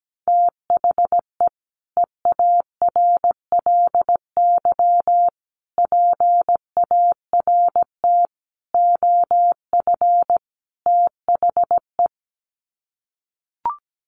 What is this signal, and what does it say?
Morse code, heard as a keyed tone, about 17 wpm, 700 Hz. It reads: THE EARLY PART OF THE
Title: Sets of 5 Words Encore2 17wpm